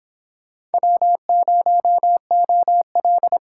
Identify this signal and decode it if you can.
Morse code, heard as a keyed tone, about 30 wpm, 700 Hz. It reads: W0OL